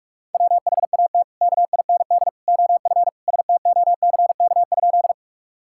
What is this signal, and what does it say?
Morse code, heard as a keyed tone, about 47 wpm, 700 Hz. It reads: WHAT KIND OF STOCK?
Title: Sentences from Top 1000 Words - Review Part 5 - 45wpm